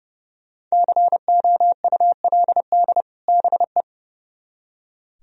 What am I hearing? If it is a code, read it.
Morse code, heard as a keyed tone, about 39 wpm, 700 Hz. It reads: COULD BE